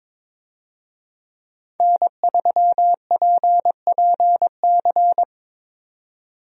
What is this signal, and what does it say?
Morse code, heard as a keyed tone, about 22 wpm, 700 Hz. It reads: N3PPC
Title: USA Call Signs - 5 Characters - 22wpm